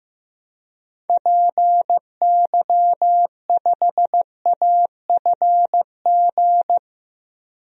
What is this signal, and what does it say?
Morse code, heard as a keyed tone, about 15 wpm, 700 Hz. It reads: PY5AFG
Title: DXCC-100 Call Signs - 6 Characters - 15wpm